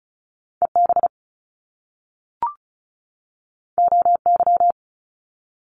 Morse code, heard as a keyed tone, about 36 wpm, 700 Hz. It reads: EB OY
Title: Top Two Letter Combinations in Words 35wpm